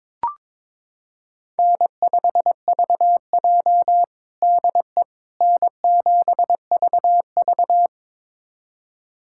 Morse code, heard as a keyed tone, about 22 wpm, 700 Hz. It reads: N5VJ DE N7VV
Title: USA Callsign DE Callsign - 4 Characters - 22wpm